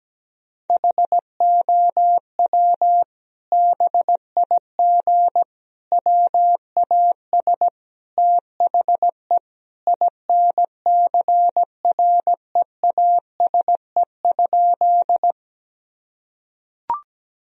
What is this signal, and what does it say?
Morse code, heard as a keyed tone, about 17 wpm, 700 Hz. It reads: HOW BIG WAS THE INCREASE?
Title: Sentences from Top 1000 Words - Review Part 4 - 17wpm